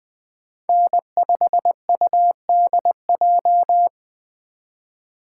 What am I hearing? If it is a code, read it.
Morse code, heard as a keyed tone, about 20 wpm, 700 Hz. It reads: N5UDJ
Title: USA Call Signs - 5 Characters - 3x Character Spacing - 20wpm